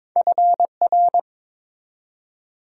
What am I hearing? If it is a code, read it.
Morse code, heard as a keyed tone, about 22 wpm, 690 Hz. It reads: FR
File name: practice / 22wpm / Two Letter-Number Combinations - 22wpm